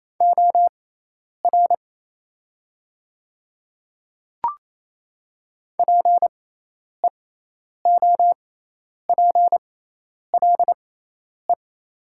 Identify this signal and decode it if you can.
Morse code, heard as a keyed tone, about 28 wpm, spaced as for 10 wpm, 700 Hz. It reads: OR PEOPLE
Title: 6x Character Spacing - Top 100 Words - 28wpm